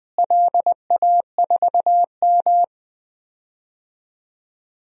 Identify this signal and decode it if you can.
Morse code, heard as a keyed tone, about 20 wpm, 700 Hz. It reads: LA4M
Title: DXCC-100 Call Signs - 4 Characters - Repeated 3x - 20wpm